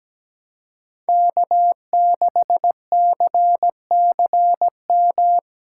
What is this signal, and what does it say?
Morse code, heard as a keyed tone, about 17 wpm, 700 Hz. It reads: K6CCM
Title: USA Callsign DE Callsign - 5 Characters - 17wpm